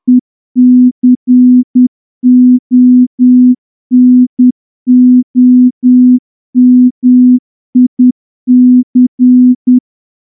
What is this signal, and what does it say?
Morse code, heard as a keyed tone, about 9 wpm, 250 Hz. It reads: ECONOMIC
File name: morse-training-10WPM